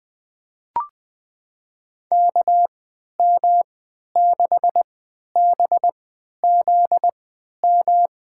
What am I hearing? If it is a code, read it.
Morse code, heard as a keyed tone, about 20 wpm, spaced as for 11 wpm, 700 Hz. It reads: KM6BZM
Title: USA Call Signs - 6 Characters - 3x Character Spacing - 20wpm